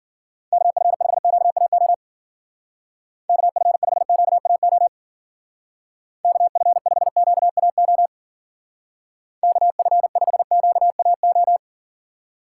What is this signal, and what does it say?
Morse code, heard as a keyed tone, about 50 wpm, 700 Hz. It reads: KF5QAO KF5QAO KF5QAO KF5QAO
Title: Speed Racing - USA 2x3 Call Signs 40wpm